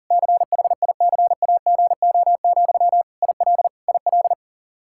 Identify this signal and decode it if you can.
Morse code, heard as a keyed tone, about 40 wpm, 700 Hz. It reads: CHICAGO, IL IL
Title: K1USN SST QSO Practice 40wpm